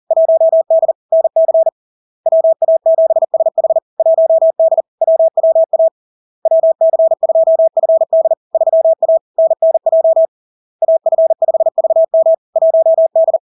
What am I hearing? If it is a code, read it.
Morse code, heard as a keyed tone, about 40 wpm, 650 Hz. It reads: 1D NC WA7SH 1D WWA WC2FD 3A NNJ AF5VM 1D
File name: LARK-FD-2022-Log-in-CW-40wpm-650-Hz